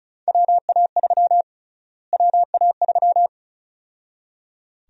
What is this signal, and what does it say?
Morse code, heard as a keyed tone, about 36 wpm, 700 Hz. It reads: WA3 WA3
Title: USA Call Sign Prefixes - Repeated 2x - 35wpm